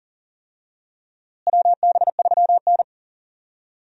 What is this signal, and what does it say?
Morse code, heard as a keyed tone, about 40 wpm, 700 Hz. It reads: WB3N
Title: USA Call Signs - 4 Characters - Repeated 2x - 40wpm